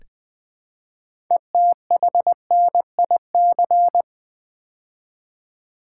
Morse code, heard as a keyed tone, about 20 wpm, 700 Hz. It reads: ETHNIC
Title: Build A Word - 2 Letters At A Time - 6-Letter Words - 20wpm